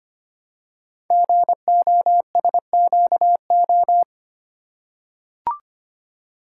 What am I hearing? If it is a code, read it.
Morse code, heard as a keyed tone, about 25 wpm, 700 Hz. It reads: GOSQO
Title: Code Groups - 5 Letters - 25wpm